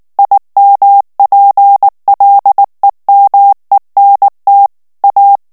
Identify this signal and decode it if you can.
Morse code, heard as a keyed tone, about 19 wpm, 800 Hz. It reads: IMPLEMENT A
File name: ANGLAIS texte clair 3 vitesse 18 mots minute